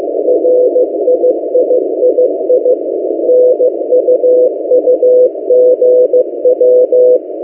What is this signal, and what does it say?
Morse code, heard as a keyed tone, about 15 wpm, 500 Hz. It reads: RIIII NUUGW
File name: M08a- -1355utc-8096khz